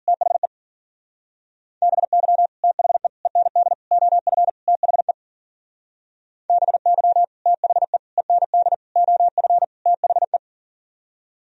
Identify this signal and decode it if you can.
Morse code, heard as a keyed tone, about 53 wpm, 700 Hz. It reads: THE BY THE END OF THE BY THE END OF THE